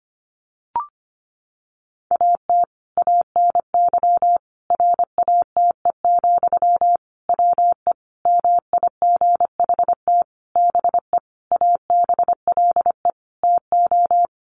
Morse code, heard as a keyed tone, about 25 wpm, 700 Hz. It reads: AT ANY RATE, WE MIGHT BE ABLE TO
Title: Sentences from Top 700 Words 25wpm